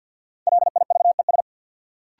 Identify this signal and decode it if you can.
Morse code, heard as a keyed tone, about 47 wpm, 700 Hz. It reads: LIVES
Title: Sets of 5 Words Encore - New Word Drill - 50wpm